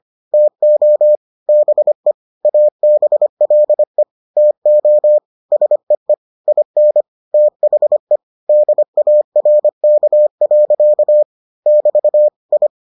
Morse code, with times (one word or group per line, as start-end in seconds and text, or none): 0.3-1.2 TO
1.5-2.1 BE
2.4-4.0 ABLE
4.4-5.2 TO
5.5-6.1 SEE
6.5-7.0 IN
7.3-8.2 THE
8.5-11.2 DARK.
11.7-12.3 <BT>
12.5-12.7 I